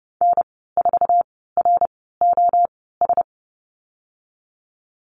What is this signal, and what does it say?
Morse code, heard as a keyed tone, about 30 wpm, spaced as for 17 wpm, 700 Hz. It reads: N4ROS
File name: Lesson 14g - Number 2 - Call Signs 30wpm